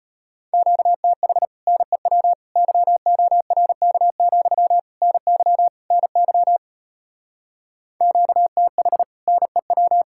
Morse code, heard as a keyed tone, about 33 wpm, 700 Hz. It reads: QTH NEW YORK, NY NY QTH NEW